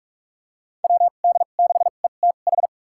Morse code, heard as a keyed tone, about 46 wpm, spaced as for 32 wpm, 700 Hz. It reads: WD6ETH
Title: USA Call Signs - 2x Character Spacing - 45wpm